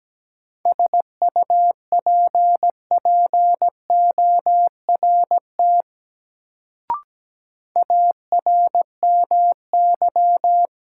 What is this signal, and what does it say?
Morse code, heard as a keyed tone, about 17 wpm, 700 Hz. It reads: SUPPORT ARMY
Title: Sentences from Top 800 Words - New Word Drill - 17wpm